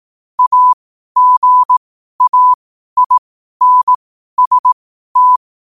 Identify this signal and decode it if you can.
Morse code, heard as a keyed tone, about 18 wpm, spaced as for 12 wpm, 1000 Hz. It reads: AGAINST
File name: qotdfiles / 12wpm